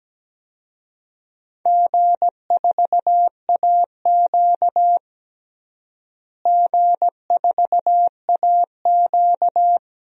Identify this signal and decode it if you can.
Morse code, heard as a keyed tone, about 17 wpm, 700 Hz. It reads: G4AQ G4AQ
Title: DXCC-100 Call Signs - 4 Characters - Repeated 3x - 17wpm